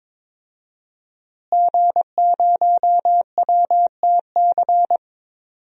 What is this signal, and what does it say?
Morse code, heard as a keyed tone, about 22 wpm, 700 Hz. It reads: G0WTC
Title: DXCC-100 Call Signs - 5 Characters - Repeated 2x - 22wpm